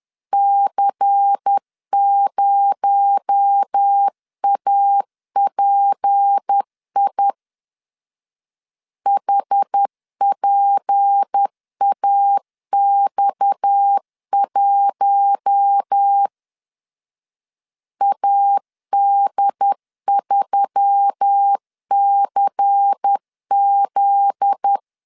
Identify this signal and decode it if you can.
Morse code, heard as a keyed tone, about 10 wpm, 800 Hz. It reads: C0API HPAX1 AD3CZ
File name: LEZIONI / 8 YJ45 / 8 YJ4 10wpm 2633